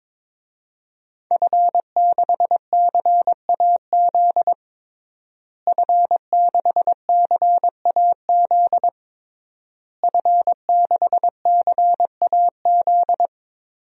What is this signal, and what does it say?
Morse code, heard as a keyed tone, about 22 wpm, 700 Hz. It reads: F6CAZ F6CAZ F6CAZ
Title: DXCC-100 Call Signs - 5 Characters - Repeated 3x - 22wpm